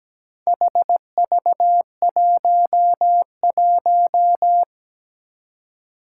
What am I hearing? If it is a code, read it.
Morse code, heard as a keyed tone, about 17 wpm, 700 Hz. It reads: HV11
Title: Four Challenging Letter-Number Combinations 17wpm